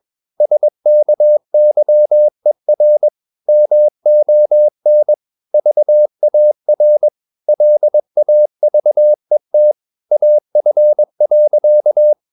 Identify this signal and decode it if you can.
Morse code, heard as a keyed tone, about 22 wpm, 600 Hz. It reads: SKYER MON VAR LAVET AF.
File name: Ferie QRS 21wpm